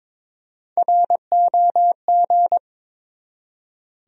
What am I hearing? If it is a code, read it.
Morse code, heard as a keyed tone, about 22 wpm, 700 Hz. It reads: ROG
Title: USA Common Names - Max 3 Letters - 22wpm